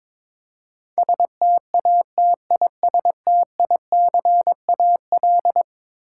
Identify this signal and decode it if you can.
Morse code, heard as a keyed tone, about 22 wpm, 700 Hz. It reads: STATISTICAL
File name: practice / 22wpm / Eleven Letter Words - 22wpm